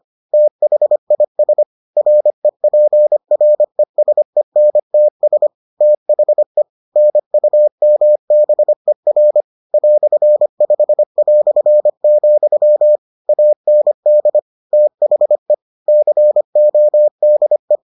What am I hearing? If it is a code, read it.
Morse code, heard as a keyed tone, about 25 wpm, 600 Hz. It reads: THIS REPRESENTS THE NUMBER "5", AND THE CODE